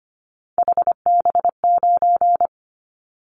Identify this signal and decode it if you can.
Morse code, heard as a keyed tone, about 25 wpm, 700 Hz. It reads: HB9
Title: Callsign Prefixes for Top 25 DXCC - 25wpm